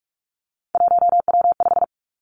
Morse code, heard as a keyed tone, about 46 wpm, 700 Hz. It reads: 1W5
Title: Three Challenging Letter-Number Combinations 45wpm